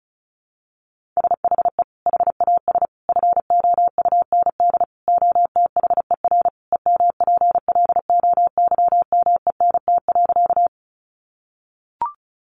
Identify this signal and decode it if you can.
Morse code, heard as a keyed tone, about 34 wpm, 700 Hz. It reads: SHE HAS FOUND OTHER EMPLOYMENT.
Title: Sentences from Top 1500 Words 35wpm